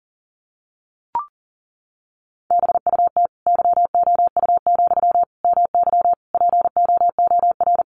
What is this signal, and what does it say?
Morse code, heard as a keyed tone, about 40 wpm, 700 Hz. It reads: BUT YOU, MY POOR
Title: Sentences from Top 1000 Words 40wpm